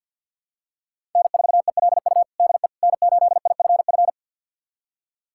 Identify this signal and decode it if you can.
Morse code, heard as a keyed tone, about 51 wpm, 700 Hz. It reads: N4ELU DE N8IFF
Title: USA Callsign DE Callsign - 5 Characters - 50wpm